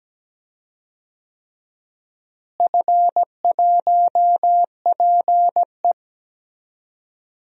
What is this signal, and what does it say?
Morse code, heard as a keyed tone, about 17 wpm, 700 Hz. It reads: F1PE
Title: DXCC-100 Call Signs - 4 Characters - Repeated 3x - 17wpm